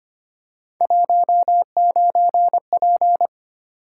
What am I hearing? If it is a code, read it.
Morse code, heard as a keyed tone, about 25 wpm, 700 Hz. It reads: 19P
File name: Three Letter-Number Combinations - 25wpm